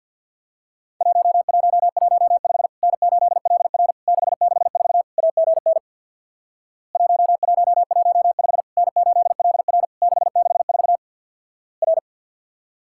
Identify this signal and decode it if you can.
Morse code, heard as a keyed tone, about 50 wpm, 700 Hz. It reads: 1115 N8LR 664 AGN 1115 N8LR 664 R
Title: Worked All Europe DX Contest - QTC Exchange with Repeats - 50wpm